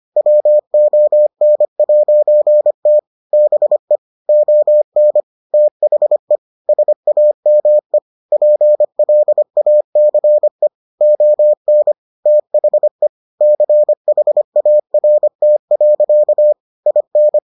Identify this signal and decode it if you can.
Morse code, heard as a keyed tone, about 25 wpm, 600 Hz. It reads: WON'T BE ON THE SAME PLACE ON THE CHART. IN